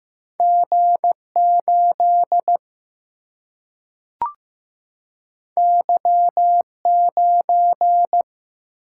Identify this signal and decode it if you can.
Morse code, heard as a keyed tone, about 15 wpm, 700 Hz. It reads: G8 Y9
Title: Single-Letter Single-Number 15wpm